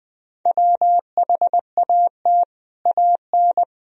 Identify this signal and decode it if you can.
Morse code, heard as a keyed tone, about 21 wpm, 700 Hz. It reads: WHAT AN
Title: Sentences from Top 1000 Words - Review Part 1 - 20wpm